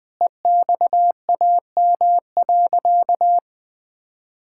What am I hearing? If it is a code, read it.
Morse code, heard as a keyed tone, about 20 wpm, 700 Hz. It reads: EXAM.
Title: Sentences from Top 1000 Words - Review Part 5 - 20wpm